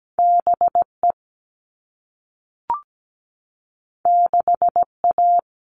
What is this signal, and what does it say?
Morse code, heard as a keyed tone, about 17 wpm, 700 Hz. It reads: BE 6A
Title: All Cut Numbers - 2-Digit 17wpm